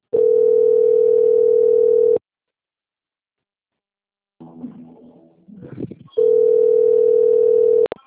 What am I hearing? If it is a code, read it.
Ringback tone (North America), 440 + 480 Hz.